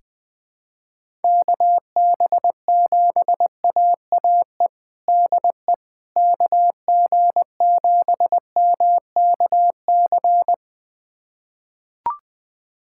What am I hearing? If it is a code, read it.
Morse code, heard as a keyed tone, about 21 wpm, 700 Hz. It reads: KB7AAE DE KG7MKC
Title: USA Callsign DE Callsign - 6 Characters - 20wpm